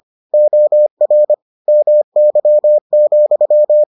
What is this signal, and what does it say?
Morse code, heard as a keyed tone, about 25 wpm, 600 Hz. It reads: OR MY,